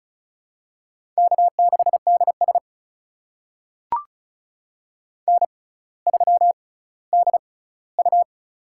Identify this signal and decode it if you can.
Morse code, heard as a keyed tone, about 34 wpm, spaced as for 18 wpm, 700 Hz. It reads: K6DS N3DU
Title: USA Call Signs - 6x Character Spacing - 35wpm